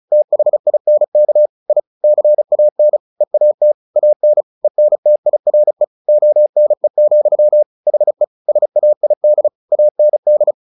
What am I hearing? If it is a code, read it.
Morse code, heard as a keyed tone, about 34 wpm, 600 Hz. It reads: THINK I CAN EAT AN ENTIRE ONE, HE SAID AND
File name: the old man and the sea / the old man and the sea 1 35wpm